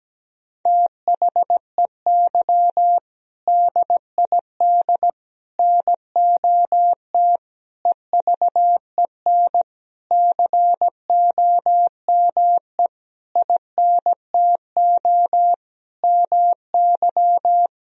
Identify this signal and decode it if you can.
Morse code, heard as a keyed tone, about 17 wpm, 700 Hz. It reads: THEY DID NOT EVEN COME INTO MY
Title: Sentences from Top 1000 Words - Review Part 1 - 17wpm